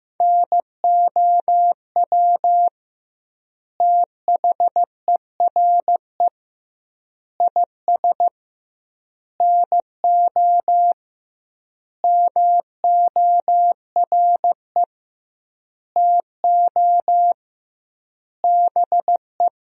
Morse code, heard as a keyed tone, about 15 wpm, 700 Hz. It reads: NOW THERE IS NO MORE TO BE
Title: 2x Word-Spacing - Sentences from Top 100 Words - 15wpm